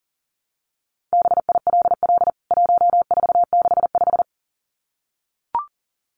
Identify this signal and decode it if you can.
Morse code, heard as a keyed tone, about 40 wpm, 700 Hz. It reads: BILL 1465
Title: CWT Exchange 40wpm